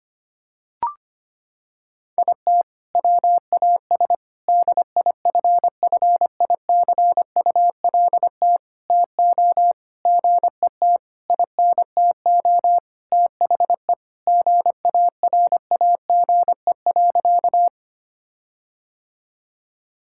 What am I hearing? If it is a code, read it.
Morse code, heard as a keyed tone, about 25 wpm, 700 Hz. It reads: IT WAS DIFFICULT TO GET INTO THE GARAGE.
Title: Sentences from Top 1600 Words 25wpm